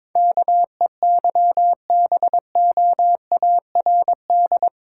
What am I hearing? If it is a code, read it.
Morse code, heard as a keyed tone, about 22 wpm, 700 Hz. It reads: KEYBOARD
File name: practice / 22wpm / Eight Letter Words - 22wpm